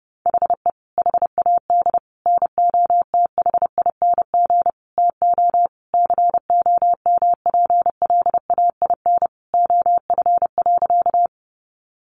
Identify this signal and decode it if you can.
Morse code, heard as a keyed tone, about 30 wpm, 700 Hz. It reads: HE HAD NOTHING TO COMPLAIN OF.
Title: Sentences from Top 1300 Words 30wpm